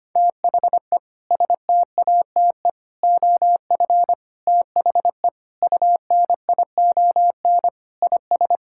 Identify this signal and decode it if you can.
Morse code, heard as a keyed tone, about 25 wpm, 700 Hz. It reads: THE STATE OF THE UNION IS
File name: practice / 25wpm / Sentences with 3-Word Phrases - 25wpm